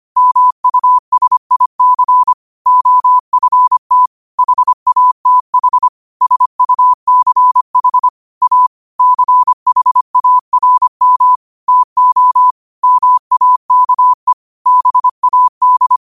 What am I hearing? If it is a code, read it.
Morse code, heard as a keyed tone, about 25 wpm, 1000 Hz. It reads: MUSIC OFT HATH SUCH A CHARM TO MAKE BAD
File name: qotdfiles / 25wpm